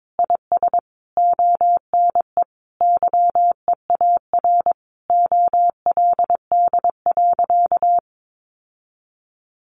Morse code, heard as a keyed tone, about 22 wpm, 700 Hz. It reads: IS ONE YEAR OLD.
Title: Sentences from Top 1400 Words 22wpm